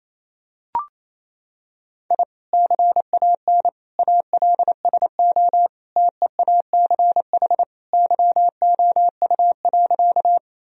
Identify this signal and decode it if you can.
Morse code, heard as a keyed tone, about 28 wpm, 700 Hz. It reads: I CAN ALSO TEACH YOU.
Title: Sentences from Top 1000 Words - Review Part 2 - 28wpm